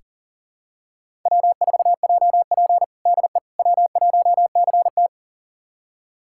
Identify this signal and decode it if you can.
Morse code, heard as a keyed tone, about 40 wpm, 700 Hz. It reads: W4JP DE W1CT
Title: USA Callsign DE Callsign - 4 Characters - 40wpm